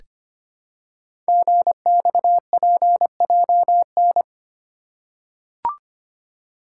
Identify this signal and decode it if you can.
Morse code, heard as a keyed tone, about 25 wpm, 700 Hz. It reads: GXPJN
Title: Code Groups - 5 Letter:Numbers - 25wpm